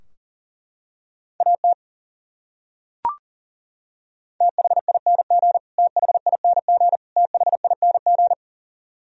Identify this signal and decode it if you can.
Morse code, heard as a keyed tone, about 40 wpm, 700 Hz. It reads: AT THING THING THING
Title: Top 100 Words Repeated 40wpm